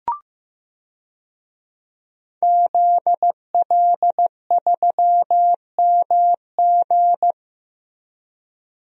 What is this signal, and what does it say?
Morse code, heard as a keyed tone, about 15 wpm, 700 Hz. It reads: ZL3MG
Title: DXCC-100 Call Signs - 5 Characters - Repeated 2x - 15wpm